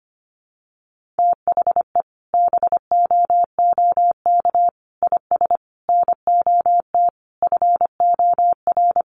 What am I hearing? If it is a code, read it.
Morse code, heard as a keyed tone, about 25 wpm, 700 Hz. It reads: THE BOOK IS NOT FOR
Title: Sentences from Top 1400 Words 25wpm